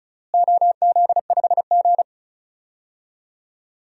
Morse code, heard as a keyed tone, about 34 wpm, 700 Hz. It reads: OZ5G
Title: DXCC-100 Call Signs - 4 Characters - Repeated 3x - 35wpm